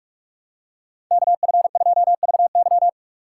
Morse code, heard as a keyed tone, about 45 wpm, 700 Hz. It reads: KF2VY
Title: USA Call Signs - 5 Characters - 6x Character Spacing - 45wpm